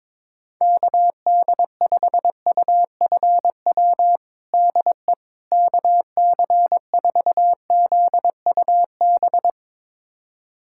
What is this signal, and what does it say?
Morse code, heard as a keyed tone, about 22 wpm, 700 Hz. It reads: KD5UFW DE KC4ZUB
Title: USA Callsign DE Callsign - 6 Characters - 22wpm